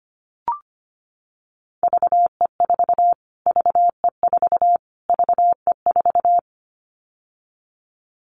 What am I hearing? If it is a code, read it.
Morse code, heard as a keyed tone, about 25 wpm, 700 Hz. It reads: VE4 VE4 VE4
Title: Callsign Prefixes for Top 25 DXCC - Repeated 3x - 25wpm